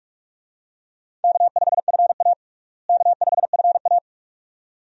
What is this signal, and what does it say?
Morse code, heard as a keyed tone, about 45 wpm, 700 Hz. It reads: K5FA K5FA
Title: USA Call Signs - 4 Characters - Repeated 2x - 45wpm